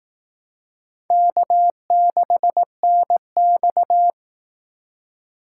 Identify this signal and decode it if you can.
Morse code, heard as a keyed tone, about 18 wpm, 700 Hz. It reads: K6NX
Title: Speed Racing - USA 1x2 Call Signs 15wpm